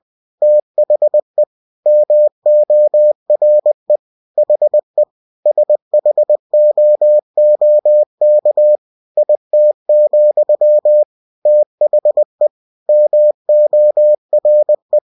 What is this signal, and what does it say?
Morse code, heard as a keyed tone, about 20 wpm, 600 Hz. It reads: THE MORE HE SHOOK IT, THE MORE
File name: cw-books / winnie-the-pooh / winnie-20wpm-page-0042